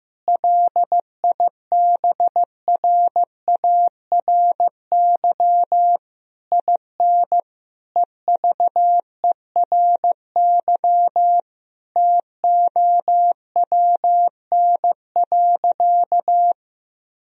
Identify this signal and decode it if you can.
Morse code, heard as a keyed tone, about 15 wpm, 700 Hz. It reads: LIBRARY IN EVERY TOWN.
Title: Sentences from Top 1000 Words - Review Part 4 - 15wpm